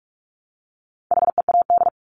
Morse code, heard as a keyed tone, about 44 wpm, 700 Hz. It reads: HEAD
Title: Top 300 Words 45wpm